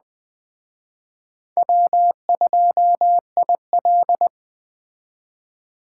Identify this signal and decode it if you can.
Morse code, heard as a keyed tone, about 20 wpm, 700 Hz. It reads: W2IL
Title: Speed Racing - USA 1x2 Call Signs 17wpm